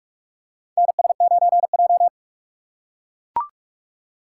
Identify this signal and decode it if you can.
Morse code, heard as a keyed tone, about 44 wpm, 700 Hz. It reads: NS9J
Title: USA Call Signs - 4 Characters - 3x Character Spacing - 45wpm